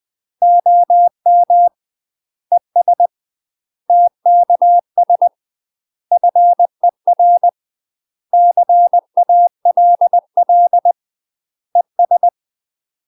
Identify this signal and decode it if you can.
Morse code, heard as a keyed tone, about 20 wpm, 700 Hz. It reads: OM ES TKS FER CALL ES
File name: UK MASTER QSOs TRAINING / 20WPM / UK MASTER QSOs 0046